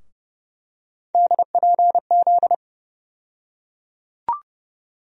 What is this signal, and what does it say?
Morse code, heard as a keyed tone, about 31 wpm, 700 Hz. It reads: DPZ
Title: USA Call Sign Postfixes - 3 Letters 30wpm